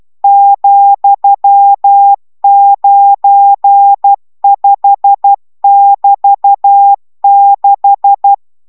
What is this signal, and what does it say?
Morse code, heard as a keyed tone, about 12 wpm, 800 Hz. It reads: ,95<BT>6